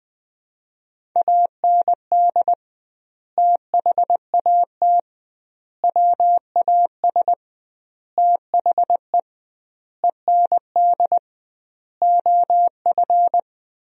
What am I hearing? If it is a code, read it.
Morse code, heard as a keyed tone, about 21 wpm, 700 Hz. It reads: AND THAT WAS THE END OF
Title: 2x Word-Spacing - Sentences from Top 200 Words - 20wpm